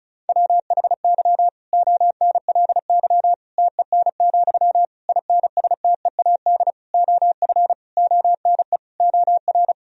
Morse code, heard as a keyed tone, about 34 wpm, 700 Hz. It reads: WHY ONLY TEN, INSTEAD OF ONE OR
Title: Sentences from Top 1000 Words - Review Part 4 - 35wpm